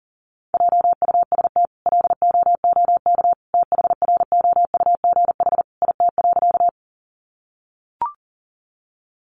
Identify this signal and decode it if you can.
Morse code, heard as a keyed tone, about 40 wpm, 700 Hz. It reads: JUST LOOK THROUGH IT.
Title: Sentences from Top 200 Words 40wpm